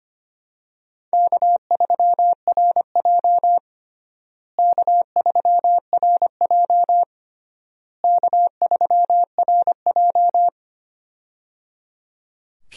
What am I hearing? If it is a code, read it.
Morse code, heard as a keyed tone, about 25 wpm, 700 Hz. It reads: K3RJ K3RJ K3RJ
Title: USA Call Signs - 4 Characters - Repeated 3x - 25wpm